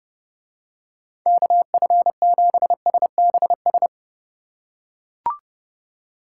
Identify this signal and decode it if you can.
Morse code, heard as a keyed tone, about 32 wpm, 700 Hz. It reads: KF7SBS